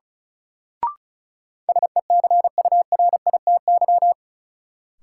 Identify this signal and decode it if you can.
Morse code, heard as a keyed tone, about 36 wpm, 700 Hz. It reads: SECURITY
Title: Sentences from Top 1400 Words - New Word Drill Alt - 35wpm